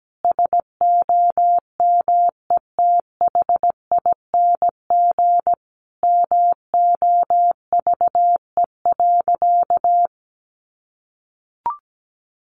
Morse code, heard as a keyed tone, about 17 wpm, 700 Hz. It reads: SOMETHING MOVE.
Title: Sentences from Top 1200 Words 17wpm